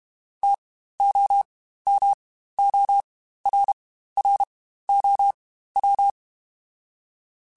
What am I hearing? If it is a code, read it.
Morse code, heard as a keyed tone, about 32 wpm, spaced as for 15 wpm, 800 Hz. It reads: TOMORROW
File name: Spaced / Chuckles Spaced-mp3s / SoftWords sp @30